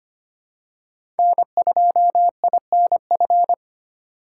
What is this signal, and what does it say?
Morse code, heard as a keyed tone, about 25 wpm, 700 Hz. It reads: N2INF
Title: USA Call Signs - 5 Characters - 2x Character Spacing - 25wpm